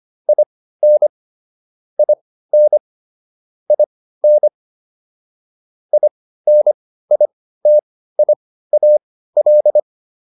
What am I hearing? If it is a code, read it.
Morse code, heard as a keyed tone, about 25 wpm, spaced as for 15 wpm, 600 Hz. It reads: IN IN IN INITIAL